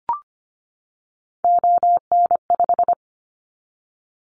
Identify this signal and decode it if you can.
Morse code, heard as a keyed tone, about 25 wpm, 700 Hz. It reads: ON5
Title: Callsign Prefixes for Top 50 DXCC - 25wpm